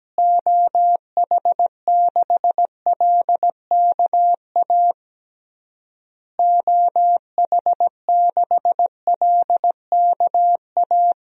Morse code, heard as a keyed tone, about 17 wpm, 700 Hz. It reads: OH6LKA OH6LKA
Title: DXCC-100 Call Signs - 6 Characters - Repeated 3x - 17wpm